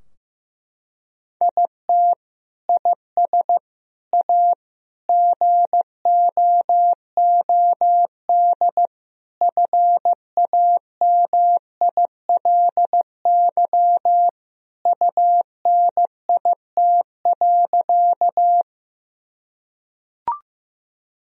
Morse code, heard as a keyed tone, about 16 wpm, 700 Hz. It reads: IT IS A GOOD FAMILY UNIT.
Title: Sentences from Top 1000 Words - Review Part 2 - 15wpm